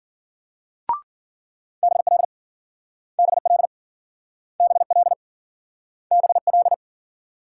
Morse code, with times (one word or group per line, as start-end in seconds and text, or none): 1.8-2.2 BL
3.2-3.7 BL
4.6-5.1 BL
6.1-6.7 BL